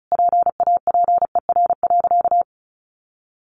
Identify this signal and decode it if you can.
Morse code, heard as a keyed tone, about 34 wpm, 700 Hz. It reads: PAPER.
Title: Sentences from Top 1200 Words 35wpm